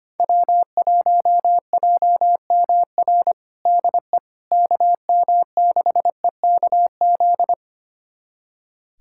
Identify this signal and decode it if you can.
Morse code, heard as a keyed tone, about 25 wpm, 700 Hz. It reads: W1JMR DE KM6EKZ
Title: USA Callsign DE Callsign 25wpm